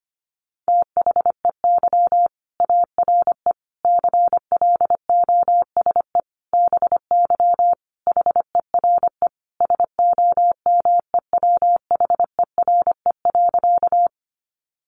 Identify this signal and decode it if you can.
Morse code, heard as a keyed tone, about 25 wpm, 700 Hz. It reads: THEY ARE CLOSE BY HERE SOMEWHERE.
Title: Sentences from Top 1400 Words 25wpm